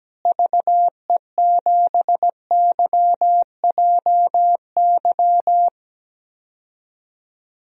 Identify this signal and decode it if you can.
Morse code, heard as a keyed tone, about 17 wpm, 700 Hz. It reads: VE7YJY
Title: DXCC-100 Call Signs - 6 Characters - Repeated 2x - 17wpm